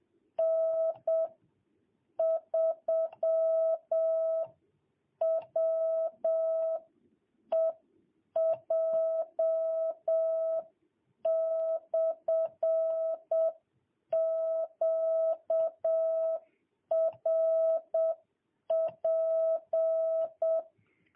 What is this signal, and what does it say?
Morse code, heard as a keyed tone, about 7 wpm, 660 Hz. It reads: N3WEJ/QRP
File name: jh8cbh / 2022cw